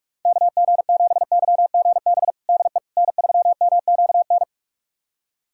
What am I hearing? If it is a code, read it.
Morse code, heard as a keyed tone, about 46 wpm, 700 Hz. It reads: KG7YGB DE N3MQN
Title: USA Callsign DE Callsign 45wpm